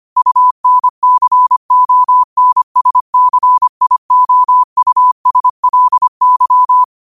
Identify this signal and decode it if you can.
Morse code, heard as a keyed tone, about 25 wpm, 1000 Hz. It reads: UNCONSCIOUSLY
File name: 25wpm